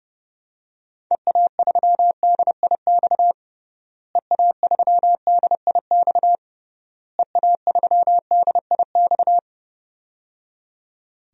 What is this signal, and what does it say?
Morse code, heard as a keyed tone, about 30 wpm, 700 Hz. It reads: EA3DIX EA3DIX EA3DIX
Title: DXCC-100 Call Signs - 6 Characters - Repeated 3x - 30wpm